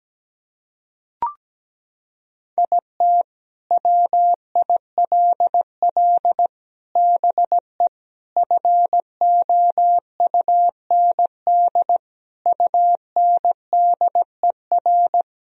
Morse code, heard as a keyed tone, about 17 wpm, 700 Hz. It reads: IT WILL BE FOUND UNDER
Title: Sentences from Top 1000 Words - Review Part 1 - 17wpm